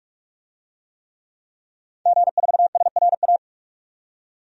Morse code, heard as a keyed tone, about 46 wpm, 700 Hz. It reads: G4SRA
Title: DXCC-100 Call Signs - 5 Characters - 45wpm